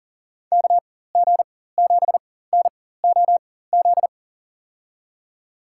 Morse code, heard as a keyed tone, about 40 wpm, spaced as for 19 wpm, 700 Hz. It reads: KG7NOZ
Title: USA Call Signs - 6 Characters - 4x Character Spacing - 40wpm